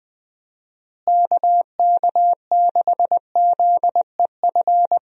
Morse code, heard as a keyed tone, about 20 wpm, 700 Hz. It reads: KK6ZEF
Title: USA Call Signs - 6 Characters - Repeated 3x - 20wpm